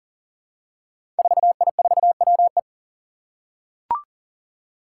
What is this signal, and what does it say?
Morse code, heard as a keyed tone, about 40 wpm, 700 Hz. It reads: 4I4WE
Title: Five Challenging Letter-Number Combinations 40wpm